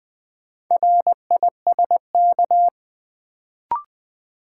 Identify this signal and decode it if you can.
Morse code, heard as a keyed tone, about 20 wpm, 700 Hz. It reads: RISK
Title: Sentences from Top 600 Words - New Word Drill - 20wpm